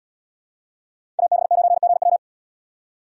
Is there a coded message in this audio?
Yes: Morse code from 1.2 to 2.2 s, heard as a keyed tone, about 77 wpm, 700 Hz. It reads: AB9GW